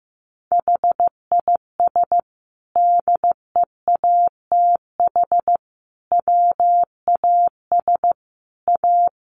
Morse code, heard as a keyed tone, about 15 wpm, 700 Hz. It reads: HIS DEATH WAS A
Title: Sentences from Top 1400 Words 15wpm